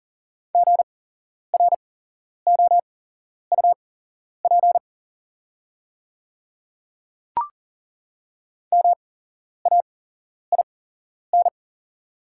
Morse code, heard as a keyed tone, about 40 wpm, spaced as for 11 wpm, 690 Hz. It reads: GROUP MAIN
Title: 8x Character Spacing - Top 300 Words - 40wpm